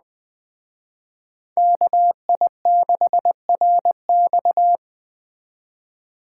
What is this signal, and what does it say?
Morse code, heard as a keyed tone, about 20 wpm, 700 Hz. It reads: KI6RX